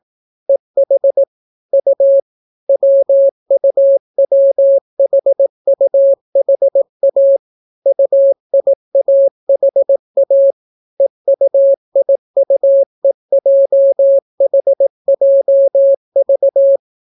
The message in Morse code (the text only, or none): EH U WUWHUHA UIAHA EUIUEJHJV